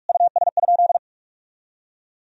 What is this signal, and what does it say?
Morse code, heard as a keyed tone, about 48 wpm, 700 Hz. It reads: US?
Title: Sentences from Top 1000 Words - Review Part 4 - 45wpm